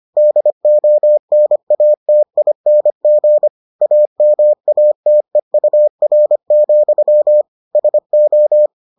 Morse code, heard as a keyed tone, about 25 wpm, 600 Hz. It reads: DONATING AMATEUR, SO